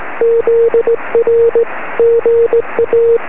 Morse code, heard as a keyed tone, about 17 wpm, 470 Hz. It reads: ZR GA
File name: sm7lcb / 21-pa0ez 23cm